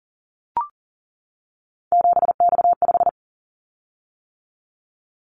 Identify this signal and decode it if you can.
Morse code, heard as a keyed tone, about 40 wpm, 700 Hz. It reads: 7X5
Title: Callsign Prefixes for Top 100 DXCC - 40wpm